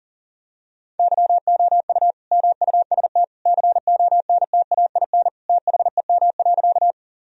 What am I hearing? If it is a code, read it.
Morse code, heard as a keyed tone, about 40 wpm, 700 Hz. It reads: YOU MUST CONTAIN THEM.
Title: Sentences from Top 1000 Words - Review Part 2 - 40wpm